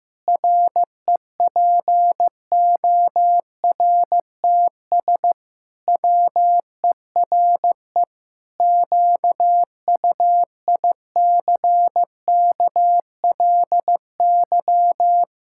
Morse code, heard as a keyed tone, about 15 wpm, 700 Hz. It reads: REPORTS WERE QUICKLY
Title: Pangram Challenge v2 - 15wpm